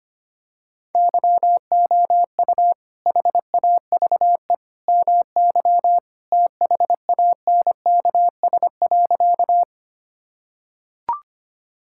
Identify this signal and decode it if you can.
Morse code, heard as a keyed tone, about 25 wpm, 700 Hz. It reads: YOU HAVE MY THANKS.
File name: Sentences with 2-Word Phrases - 25wpm